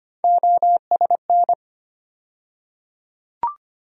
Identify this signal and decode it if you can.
Morse code, heard as a keyed tone, about 25 wpm, 700 Hz. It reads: OSN
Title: USA Call Sign Postfixes - 3 Letters 25wpm